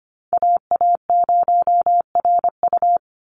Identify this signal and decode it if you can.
Morse code, heard as a keyed tone, about 25 wpm, 700 Hz. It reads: AA0RU
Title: USA Call Signs 25wpm